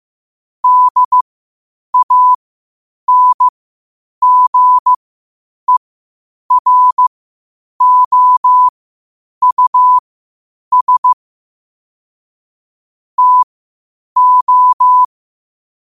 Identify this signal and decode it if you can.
Morse code, heard as a keyed tone, about 15 wpm, spaced as for 8 wpm, 1000 Hz. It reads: DANGEROUS TO
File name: qotdfiles / 8wpm